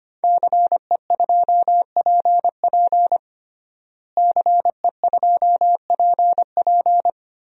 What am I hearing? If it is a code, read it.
Morse code, heard as a keyed tone, about 25 wpm, 700 Hz. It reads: CE2PP CE2PP
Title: DXCC-100 Call Signs - 5 Characters - Repeated 3x - 25wpm